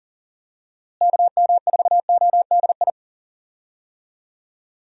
Morse code, heard as a keyed tone, about 40 wpm, 700 Hz. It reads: KM4ODI
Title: USA Call Signs - 6 Characters - Repeated 2x - 40wpm